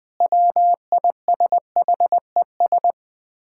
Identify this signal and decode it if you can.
Morse code, heard as a keyed tone, about 21 wpm, 700 Hz. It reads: WISHES
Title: Common QSO Phrases - 2x Word-Spacing - 20wpm